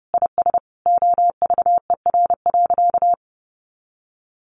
Morse code, heard as a keyed tone, about 30 wpm, 700 Hz. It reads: IS OVER.
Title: Sentences from Top 1200 Words 30wpm